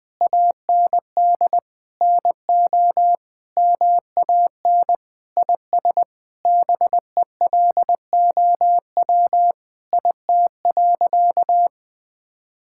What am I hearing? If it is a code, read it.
Morse code, heard as a keyed tone, about 20 wpm, 700 Hz. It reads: AND NO MAN IS BELOW IT.
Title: Sentences from Top 1000 Words - Review Part 5 - 20wpm